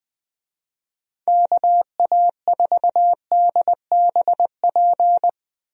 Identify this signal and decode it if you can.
Morse code, heard as a keyed tone, about 20 wpm, 700 Hz. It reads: KA4DBP
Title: USA Call Signs - 6 Characters - 20wpm